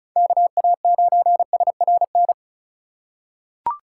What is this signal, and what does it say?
Morse code, heard as a keyed tone, about 34 wpm, 700 Hz. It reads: KA9SRN
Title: USA Call Signs - 6 Characters - 4x Character Spacing - 35wpm